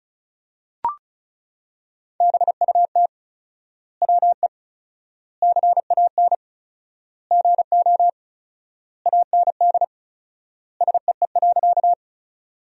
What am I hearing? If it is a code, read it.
Morse code, heard as a keyed tone, about 34 wpm, 700 Hz. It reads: BUT WE CAN GO AND SEE.
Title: 4x Word-Spacing - Sentences from Top 100 Words - 35wpm